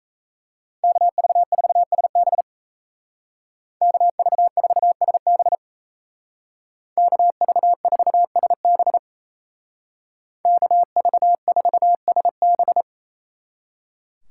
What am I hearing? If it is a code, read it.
Morse code, heard as a keyed tone, about 34 wpm, 700 Hz. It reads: KV4SB KV4SB KV4SB KV4SB
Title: Speed Racing - USA 2x2 Call Signs 28wpm